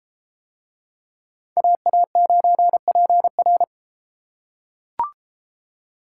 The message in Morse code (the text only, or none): AA9PR